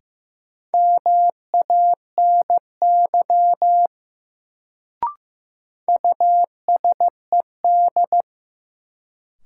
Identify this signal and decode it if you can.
Morse code, heard as a keyed tone, about 15 wpm, 700 Hz. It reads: MANY USED
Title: Sets of 2 Words - New Word Drill - 15wpm